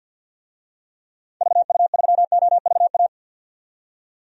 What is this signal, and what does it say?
Morse code, heard as a keyed tone, about 45 wpm, 700 Hz. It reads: VU3OVA
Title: DXCC-100 Call Signs - 6 Characters - 50wpm